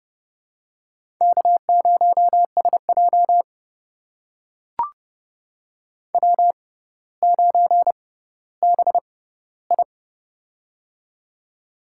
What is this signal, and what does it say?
Morse code, heard as a keyed tone, about 30 wpm, spaced as for 15 wpm, 700 Hz. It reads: K0SJ W9BI
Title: USA Call Signs - 4 Characters - 6x Character Spacing - 30wpm